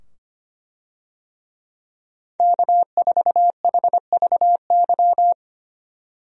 Morse code, heard as a keyed tone, about 25 wpm, 700 Hz. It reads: K4HVY
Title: USA Call Signs - 5 Characters - 25wpm